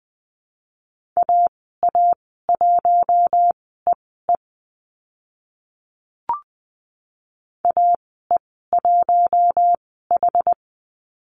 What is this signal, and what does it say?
Morse code, heard as a keyed tone, about 20 wpm, spaced as for 14 wpm, 700 Hz. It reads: AA1EE AE1H
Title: 082 Lesson 14g - Number 2 - Call Signs 20wpm